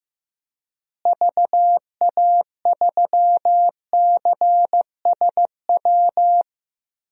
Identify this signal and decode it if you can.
Morse code, heard as a keyed tone, about 15 wpm, 700 Hz. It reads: VA3CSW
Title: DXCC-100 Call Signs - 6 Characters - Repeated 2x - 15wpm